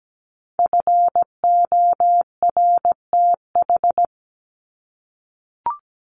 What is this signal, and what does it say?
Morse code, heard as a keyed tone, about 17 wpm, 700 Hz. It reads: FORTH
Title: Top 1751-2000 Words 17wpm